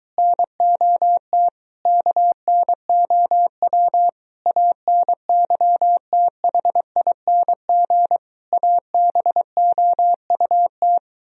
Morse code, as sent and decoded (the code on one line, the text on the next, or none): -. --- - / -.- -. --- .-- / .- -. -.-- - .... .. -. --. / .- -... --- ..- -
NOT KNOW ANYTHING ABOUT